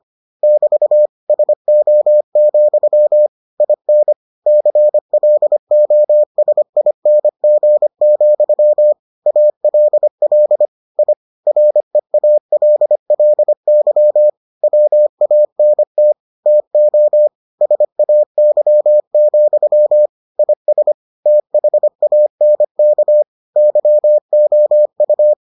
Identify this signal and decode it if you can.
Morse code, heard as a keyed tone, about 25 wpm, 600 Hz. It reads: <BT> SO, IN CLOSING, ALL I REALLY WANT TO SAY, IS THANK YOU